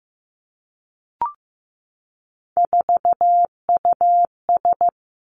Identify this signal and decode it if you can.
Morse code, heard as a keyed tone, about 15 wpm, 700 Hz. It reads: 4US
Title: Three Challenging Letter-Number Combinations 15wpm